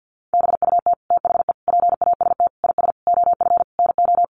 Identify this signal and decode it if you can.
Morse code, heard as a keyed tone, about 50 wpm, 700 Hz. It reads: BUT THE PAST IS OF NO
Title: Sentences from Top 1200 Words 50wpm